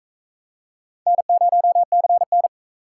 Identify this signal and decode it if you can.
Morse code, heard as a keyed tone, about 44 wpm, 700 Hz. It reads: N0CN